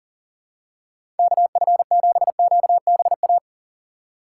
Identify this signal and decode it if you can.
Morse code, heard as a keyed tone, about 40 wpm, 700 Hz. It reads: KF7QBA
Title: USA Call Signs - 6 Characters - 6x Character Spacing - 40wpm